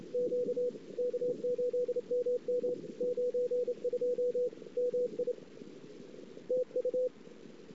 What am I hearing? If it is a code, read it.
Morse code, heard as a keyed tone, about 29 wpm, 500 Hz, with radio noise. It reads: Q K8MM 92 MI TU
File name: ss-cw